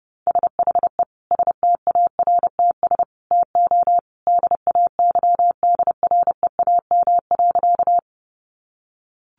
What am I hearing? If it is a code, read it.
Morse code, heard as a keyed tone, about 30 wpm, 700 Hz. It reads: SHE STARTS TO DAYDREAM.
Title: Easy Phrases 30wpm